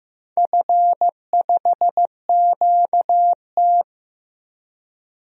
Morse code, heard as a keyed tone, about 15 wpm, 700 Hz. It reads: F5QT
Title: DXCC-100 Call Signs - 4 Characters - Repeated 3x - 15wpm